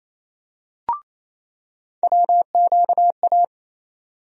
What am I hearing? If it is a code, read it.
Morse code, heard as a keyed tone, about 28 wpm, 700 Hz. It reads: WQA
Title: USA Call Sign Postfixes - 3 Letters 28wpm